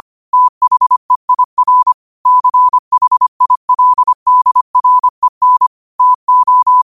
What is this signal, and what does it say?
Morse code, heard as a keyed tone, about 25 wpm, 1000 Hz. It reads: THEIR CHILDREN TO
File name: qotdfiles / 25wpm